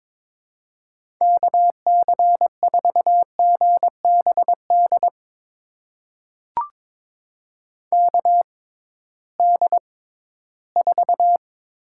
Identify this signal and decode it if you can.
Morse code, heard as a keyed tone, about 22 wpm, spaced as for 14 wpm, 700 Hz. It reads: KC4GBD KD4
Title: USA Call Signs - 6 Characters - 6x Character Spacing - 22wpm